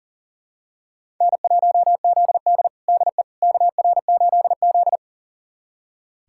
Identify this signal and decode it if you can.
Morse code, heard as a keyed tone, about 40 wpm, 700 Hz. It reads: N1ZD DE KR8Z
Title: USA Callsign DE Callsign - 4 Characters - 40wpm